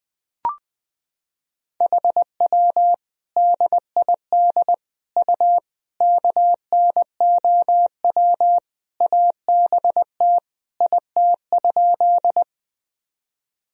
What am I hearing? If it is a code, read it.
Morse code, heard as a keyed tone, about 20 wpm, 700 Hz. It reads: HW DID U KNOW ABT IT?
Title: Sentences from Top 100 Words and Abbreviations 20wpm